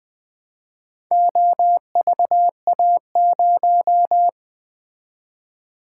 Morse code, heard as a keyed tone, about 20 wpm, 700 Hz. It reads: OVA0
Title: Four Letter-Number Combinations - 20wpm